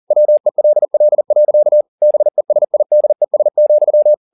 Morse code, heard as a keyed tone, about 40 wpm, 600 Hz. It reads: WELL. BESIDES,
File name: the old man and the sea 2 40wpm